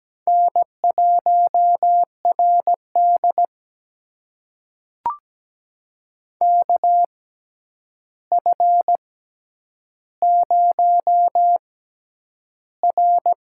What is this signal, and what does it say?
Morse code, heard as a keyed tone, about 17 wpm, spaced as for 9 wpm, 700 Hz. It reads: N1RD KF0R
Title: USA Call Signs - 4 Characters - 6x Character Spacing - 17wpm